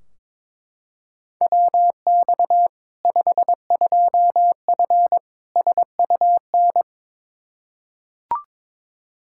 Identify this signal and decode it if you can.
Morse code, heard as a keyed tone, about 22 wpm, 700 Hz. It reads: WX 52F SUN
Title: Practice QSO - Exchange 2 - Info - 22wpm